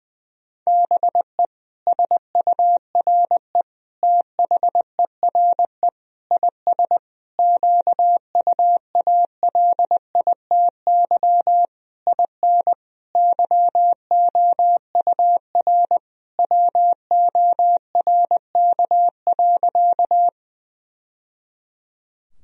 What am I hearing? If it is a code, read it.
Morse code, heard as a keyed tone, about 20 wpm, 700 Hz. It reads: BE SURE THERE IS QUALITY IN YOUR WORK.
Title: Sentences from Top 1000 Words - Review Part 4 - 20wpm